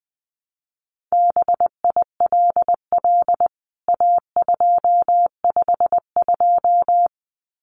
Morse code, heard as a keyed tone, about 20 wpm, 700 Hz. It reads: BILL A252
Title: CWT Exchange 20wpm